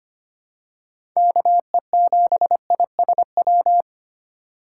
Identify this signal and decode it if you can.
Morse code, heard as a keyed tone, about 25 wpm, 700 Hz. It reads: KE7ISW
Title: USA Call Signs - 6 Characters - 2x Character Spacing - 25wpm